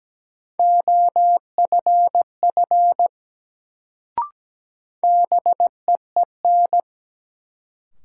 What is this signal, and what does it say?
Morse code, heard as a keyed tone, about 17 wpm, 700 Hz. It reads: OFF BEEN
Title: Sets of 2 Words - New Word Drill - 17wpm